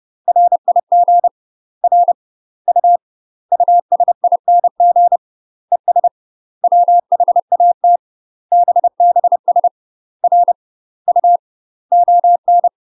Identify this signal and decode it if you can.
Morse code, heard as a keyed tone, about 30 wpm, 700 Hz. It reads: RIG R U USING ES WHAT BBS R U ON
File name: UK MASTER QSOs TRAINING / 30WPM / UK MASTER QSOs 0038